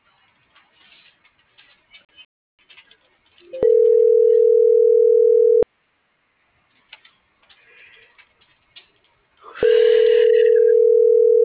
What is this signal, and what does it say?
Ringback tone (North America), 440 + 480 Hz.